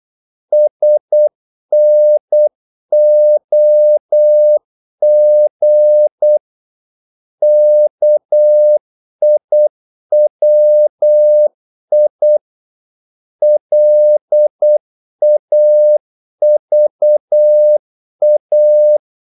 Morse code, heard as a keyed tone, about 8 wpm, 600 Hz. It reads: SNOG KIWI LAVA
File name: Korte ord DK 8wpm